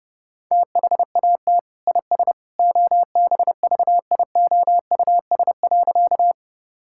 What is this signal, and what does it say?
Morse code, heard as a keyed tone, about 30 wpm, 700 Hz. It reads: THAT IS OBVIOUS.
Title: Sentences from Top 1600 Words 30wpm